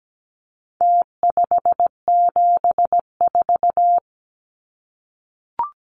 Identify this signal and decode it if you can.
Morse code, heard as a keyed tone, about 17 wpm, 700 Hz. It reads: T574
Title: Numbers - Contest Style - 4-Digit 17wpm